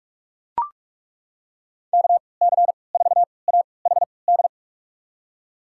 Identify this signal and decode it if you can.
Morse code, heard as a keyed tone, about 43 wpm, spaced as for 25 wpm, 690 Hz. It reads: KC4AHD
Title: USA Call Signs - 6 Characters - 3x Character Spacing - 45wpm